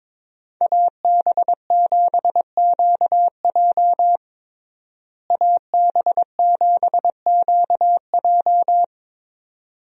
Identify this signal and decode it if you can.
Morse code, heard as a keyed tone, about 22 wpm, 700 Hz. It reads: AB7QJ AB7QJ
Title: USA Call Signs - 5 Characters - Repeated 2x - 22wpm